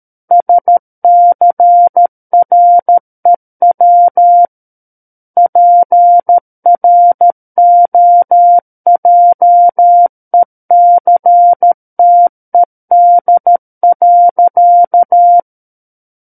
Morse code, heard as a keyed tone, about 13 wpm, 700 Hz. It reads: SCREW PROJECTED.
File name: Book One / 04-Chapter 4 - 13 WPM